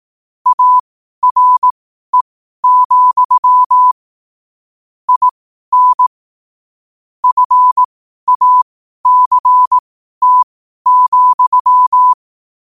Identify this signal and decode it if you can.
Morse code, heard as a keyed tone, about 18 wpm, spaced as for 12 wpm, 1000 Hz. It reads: ARE, IN FACT,
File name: qotdfiles / 12wpm